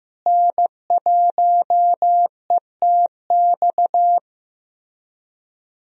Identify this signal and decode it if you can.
Morse code, heard as a keyed tone, about 15 wpm, 700 Hz. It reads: N1ETX
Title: USA Call Signs - 5 Characters - Repeated 3x - 15wpm